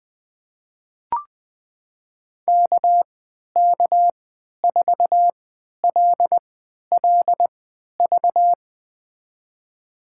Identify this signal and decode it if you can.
Morse code, heard as a keyed tone, about 20 wpm, spaced as for 11 wpm, 700 Hz. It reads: KK4LLV
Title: USA Call Signs - 3x Character Spacing - 20wpm